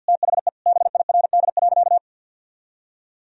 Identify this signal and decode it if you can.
Morse code, heard as a keyed tone, about 50 wpm, 700 Hz. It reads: THE BIRD.
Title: Sentences from Top 1000 Words - Review Part 2 - 50wpm